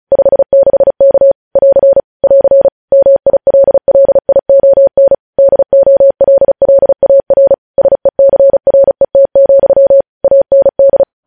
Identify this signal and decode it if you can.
Morse code, heard as a keyed tone, about 36 wpm, 550 Hz. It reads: 56K <AR> <AR> MILLION DOLLAR SECRET, AND
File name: podcast / News headlines-1x-35x35